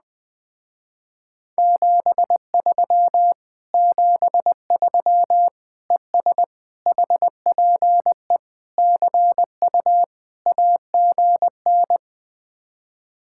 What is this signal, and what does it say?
Morse code, heard as a keyed tone, about 20 wpm, 700 Hz. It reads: 73 73 ES HPE CU AGN
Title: Speed-Racing Common QSO Phrases - 17wpm